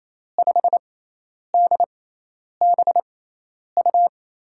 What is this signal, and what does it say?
Morse code, heard as a keyed tone, about 28 wpm, spaced as for 10 wpm, 700 Hz. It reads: 5DBU
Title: USA Call Signs - 6 Characters - 6x Character Spacing - 28wpm